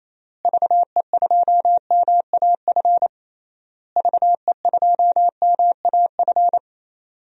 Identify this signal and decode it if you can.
Morse code, heard as a keyed tone, about 28 wpm, 700 Hz. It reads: VE2MAF VE2MAF
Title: DXCC-100 Call Signs - 6 Characters - Repeated 3x - 28wpm